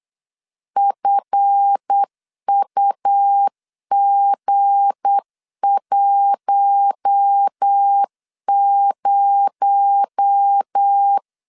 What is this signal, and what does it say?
Morse code, heard as a keyed tone, about 8 wpm, 800 Hz. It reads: FUG10
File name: LEZIONI / 10 SIGNS / 10B INT 8wpm 3534